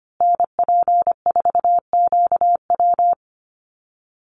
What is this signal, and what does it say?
Morse code, heard as a keyed tone, about 25 wpm, 700 Hz. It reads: NP4QW
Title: USA Call Signs Repeated 3x 25wpm